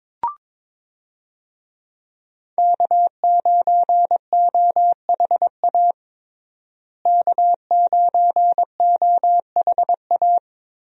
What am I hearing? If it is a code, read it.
Morse code, heard as a keyed tone, about 22 wpm, 700 Hz. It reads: K9OHA K9OHA
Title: USA Call Signs - 5 Characters - Repeated 3x - 22wpm